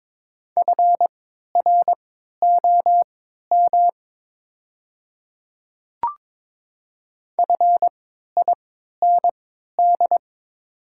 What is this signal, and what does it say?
Morse code, heard as a keyed tone, about 23 wpm, spaced as for 13 wpm, 690 Hz. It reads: FROM FIND
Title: 3x Character Spacing - Top 100 Words - 22wpm